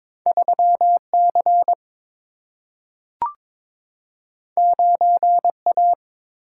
Morse code, heard as a keyed tone, about 22 wpm, 700 Hz. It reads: 3C 9A
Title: Single-Letter Single-Number 22wpm